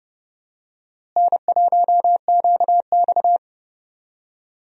Morse code, heard as a keyed tone, about 30 wpm, 700 Hz. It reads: N1QX
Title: USA Call Signs - 4 Characters - Repeated 3x - 30wpm